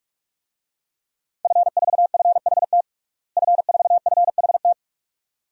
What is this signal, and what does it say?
Morse code, heard as a keyed tone, about 47 wpm, 700 Hz. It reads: F4FHT F4FHT
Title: DXCC-100 Call Signs - 5 Characters - Repeated 2x - 45wpm